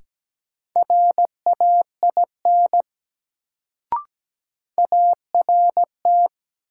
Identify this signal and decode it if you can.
Morse code, heard as a keyed tone, about 17 wpm, 700 Hz. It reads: RAIN ART
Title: Sentences from Top 600 Words - New Word Drill - 17wpm